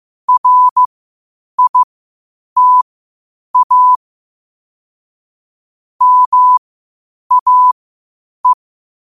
Morse code, heard as a keyed tone, about 15 wpm, spaced as for 8 wpm, 1000 Hz. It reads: RITA MAE
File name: qotdfiles / 8wpm